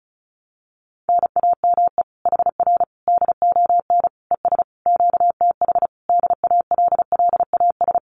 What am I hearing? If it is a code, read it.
Morse code, heard as a keyed tone, about 34 wpm, 700 Hz. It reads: NAME HR DON ES QTH DALLAS
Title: Common QSO Phrases Encore 35wpm